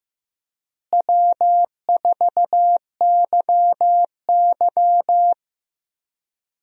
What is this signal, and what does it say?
Morse code, heard as a keyed tone, about 15 wpm, 700 Hz. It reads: W4YY
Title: USA Call Signs - 4 Characters - Repeated 3x - 15wpm